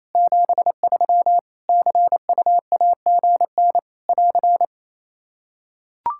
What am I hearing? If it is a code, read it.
Morse code, heard as a keyed tone, about 28 wpm, 700 Hz. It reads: 73 CUAGN <AR>
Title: Practice QSO - Exchange 4 - Ending - 28wpm